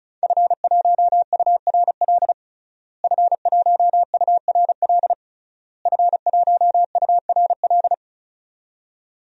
Morse code, heard as a keyed tone, about 36 wpm, 700 Hz. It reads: F1URL F1URL F1URL
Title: DXCC-100 Call Signs - 5 Characters - Repeated 3x - 35wpm